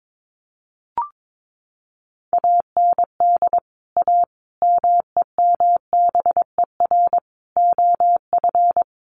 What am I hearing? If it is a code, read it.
Morse code, heard as a keyed tone, about 22 wpm, 700 Hz. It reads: AND A MEMBER OF
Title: Sets of 5 Words 22wpm